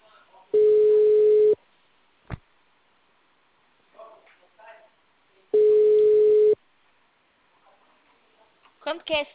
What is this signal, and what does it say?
Ringback tone (Europe, 425 Hz), with speech.